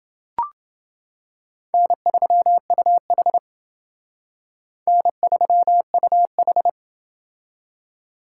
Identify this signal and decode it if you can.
Morse code, heard as a keyed tone, about 28 wpm, 700 Hz. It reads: N3UH N3UH